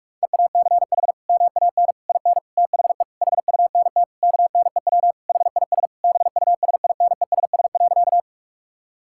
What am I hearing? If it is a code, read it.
Morse code, heard as a keyed tone, about 42 wpm, 700 Hz. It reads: EACH MAN IN THE HUNT KNEW HIS BUSINESS.
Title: Sentences from Top 1600 Words 45wpm